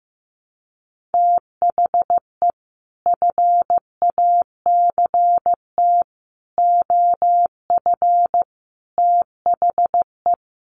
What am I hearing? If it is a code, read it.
Morse code, heard as a keyed tone, about 14 wpm, 700 Hz. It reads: THE FACT OF THE
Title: Sets of 5 Words 15wpm